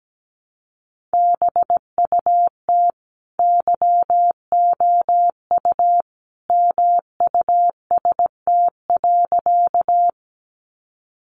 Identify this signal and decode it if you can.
Morse code, heard as a keyed tone, about 17 wpm, 700 Hz. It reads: BUT YOU MUST.
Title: Sentences from Top 200 Words 17wpm